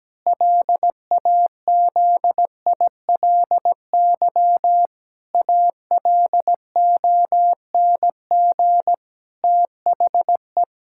Morse code, heard as a keyed tone, about 17 wpm, 700 Hz. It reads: LAZILY ALONG THE
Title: Pangram Challenge v2 - 17wpm